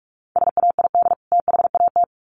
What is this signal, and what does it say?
Morse code, heard as a keyed tone, about 44 wpm, 700 Hz. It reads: SAID THAT
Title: Sets of 2 Words 45wpm